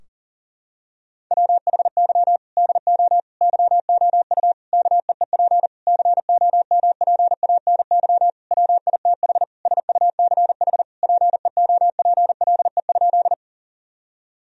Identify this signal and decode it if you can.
Morse code, heard as a keyed tone, about 40 wpm, 700 Hz. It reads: WHY DO YOU KEEP COMPANY WITH SUCH PEOPLE?
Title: Sentences from Top 1000 Words - Review Part 3 - 40wpm